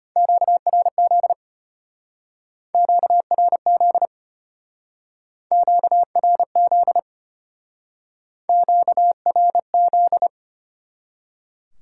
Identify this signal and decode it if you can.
Morse code, heard as a keyed tone, about 31 wpm, 700 Hz. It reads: QRZ QRZ QRZ QRZ